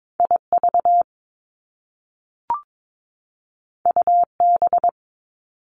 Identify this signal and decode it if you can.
Morse code, heard as a keyed tone, about 22 wpm, 700 Hz. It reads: IV UB
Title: Top Two Letter Combinations in Words 22wpm